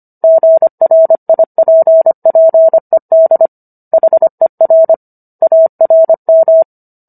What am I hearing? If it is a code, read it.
Morse code, heard as a keyed tone, about 25 wpm, 650 Hz. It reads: GRIPPED HER ARM